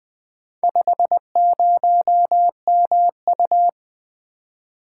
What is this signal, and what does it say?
Morse code, heard as a keyed tone, about 20 wpm, 700 Hz. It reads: 50MU
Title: Four Letter-Number Combinations - 20wpm